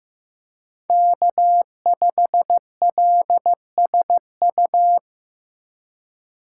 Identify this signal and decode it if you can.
Morse code, heard as a keyed tone, about 18 wpm, 700 Hz. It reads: K5LSU
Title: Speed Racing - USA 1x3 Call Signs 15wpm